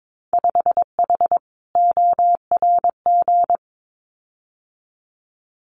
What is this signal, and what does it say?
Morse code, heard as a keyed tone, about 22 wpm, 700 Hz. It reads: 5H ORG
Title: Winter Field Day Exchanges Encore 22wpm